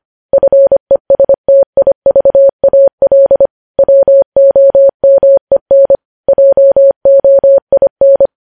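Morse code, heard as a keyed tone, about 25 wpm, 550 Hz. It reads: FESTIVAL WOMEN JOIN
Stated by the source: Morse code news headlines sent at 25 wpm.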